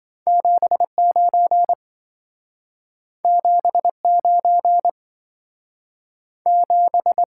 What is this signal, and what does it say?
Morse code, heard as a keyed tone, about 27 wpm, 700 Hz. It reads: 79 79 7